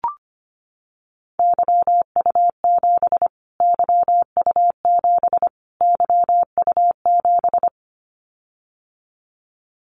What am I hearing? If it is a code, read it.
Morse code, heard as a keyed tone, about 25 wpm, 700 Hz. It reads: YU7 YU7 YU7
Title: Callsign Prefixes for Top 25 DXCC - Repeated 3x - 25wpm